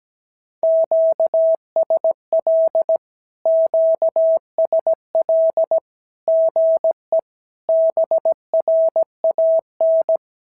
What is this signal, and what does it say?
Morse code, heard as a keyed tone, about 17 wpm, 650 Hz. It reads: QSL QSL GE BRAN
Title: SOTA QSO Practice 17wpm